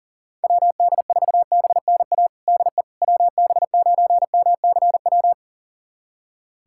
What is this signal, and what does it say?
Morse code, heard as a keyed tone, about 40 wpm, 700 Hz. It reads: WD4BNA DE WB9MCW
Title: USA Callsign DE Callsign - 6 Characters - 40wpm